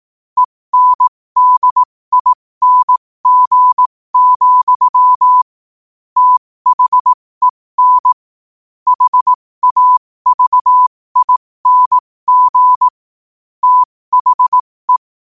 Morse code, heard as a keyed tone, about 18 wpm, spaced as for 15 wpm, 1000 Hz. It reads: ENDING, THEN HAVING THE
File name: qotdfiles / 15wpm